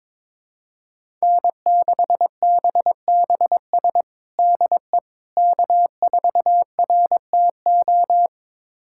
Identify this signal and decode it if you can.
Morse code, heard as a keyed tone, about 22 wpm, 700 Hz. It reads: N6BBS DE K4RTO
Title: USA Callsign DE Callsign - 5 Characters - 22wpm